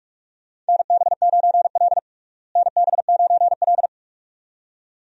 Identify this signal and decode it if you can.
Morse code, heard as a keyed tone, about 48 wpm, 700 Hz. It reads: NB9L NB9L
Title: USA Call Signs - 4 Characters - Repeated 2x - 45wpm